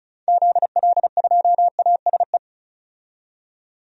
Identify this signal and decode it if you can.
Morse code, heard as a keyed tone, about 36 wpm, 700 Hz. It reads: ZL2ASE
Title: DXCC-100 Call Signs - 6 Characters - 35wpm